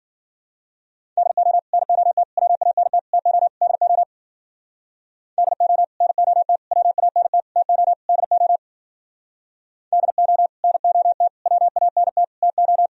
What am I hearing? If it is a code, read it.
Morse code, heard as a keyed tone, about 52 wpm, 700 Hz. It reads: DO NOT WANT TO DO DO NOT WANT TO DO DO NOT WANT TO